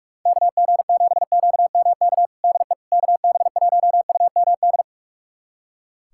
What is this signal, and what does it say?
Morse code, heard as a keyed tone, about 47 wpm, 700 Hz. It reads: KG7QMK DE KB1UMD
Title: USA Callsign DE Callsign - 6 Characters - 45wpm